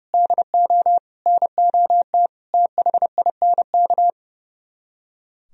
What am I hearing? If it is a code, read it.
Morse code, heard as a keyed tone, about 37 wpm, 700 Hz. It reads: DO NOT THINK
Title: Speed-Racing - Sets of 5 Words 30wpm